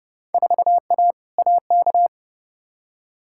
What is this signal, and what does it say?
Morse code, heard as a keyed tone, about 30 wpm, 700 Hz. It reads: 4A AK
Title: Field Day Exchanges Encore - 30wpm